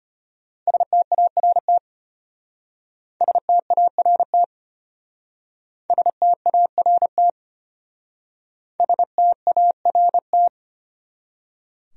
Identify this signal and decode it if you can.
Morse code, heard as a keyed tone, about 34 wpm, 700 Hz. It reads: START START START START